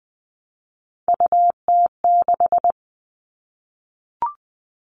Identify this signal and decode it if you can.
Morse code, heard as a keyed tone, about 20 wpm, 700 Hz. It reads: UT6
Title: All Cut Numbers - 3-Digit 20wpm